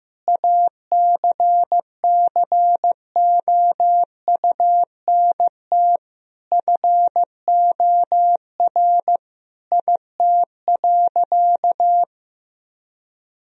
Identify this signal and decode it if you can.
Morse code, heard as a keyed tone, about 15 wpm, 700 Hz. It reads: ACCOUNT FOR IT.
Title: Sentences from Top 1000 Words - Review Part 2 - 15wpm